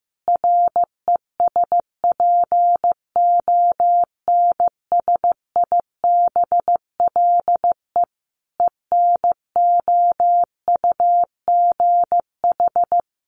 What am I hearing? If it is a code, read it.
Morse code, heard as a keyed tone, about 15 wpm, 700 Hz. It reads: RESPONSIBLE ENOUGH
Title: Additional Easy Phrases 15wpm